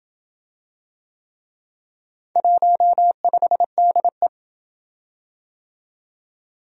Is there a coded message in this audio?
Yes: Morse code from 2.4 to 4.3 s, heard as a keyed tone, about 26 wpm, 700 Hz. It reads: 15DE